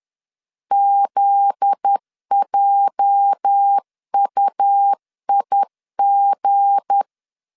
Morse code, heard as a keyed tone, about 10 wpm, 800 Hz. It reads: ZJUIG
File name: LEZIONI / 8 YJ45 / 8 YJ 10wpm 2651